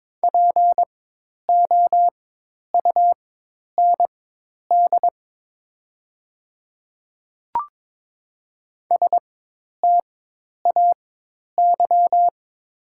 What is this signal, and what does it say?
Morse code, heard as a keyed tone, about 23 wpm, spaced as for 10 wpm, 700 Hz. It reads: POUND STAY
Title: 4x Character Spacing - Top 500 Words - 22wpm